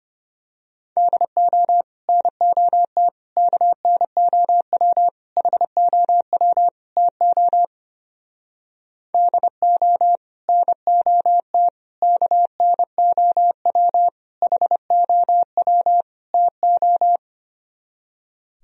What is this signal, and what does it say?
Morse code, heard as a keyed tone, about 28 wpm, 700 Hz. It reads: DO NOT KNOW HOW TO DO NOT KNOW HOW TO
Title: Speed-Racing - Sets of 5 Words 25wpm